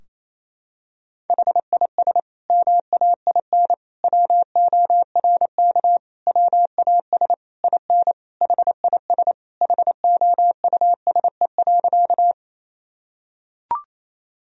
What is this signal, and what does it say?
Morse code, heard as a keyed tone, about 27 wpm, 700 Hz. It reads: HIS MAIN WORK WAS IN HIS HOUSE.
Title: Sentences from Top 1000 Words - Review Part 2 - 28wpm